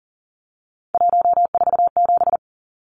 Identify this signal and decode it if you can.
Morse code, heard as a keyed tone, about 40 wpm, 700 Hz. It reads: 147
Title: Numbers - 3-Digit 40wpm